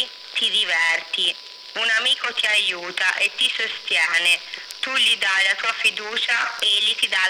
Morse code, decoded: none